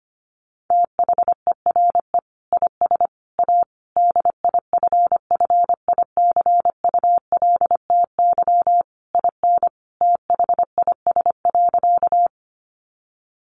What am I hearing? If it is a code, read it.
Morse code, heard as a keyed tone, about 25 wpm, 700 Hz. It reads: THERE IS A DIFFICULTY IN THIS.
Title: Sentences from Top 1500 Words 25wpm